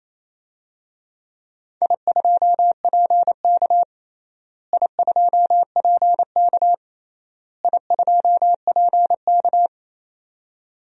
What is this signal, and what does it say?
Morse code, heard as a keyed tone, about 28 wpm, 700 Hz. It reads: I2PK I2PK I2PK
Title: DXCC-100 Call Signs - 4 Characters - Repeated 3x - 28wpm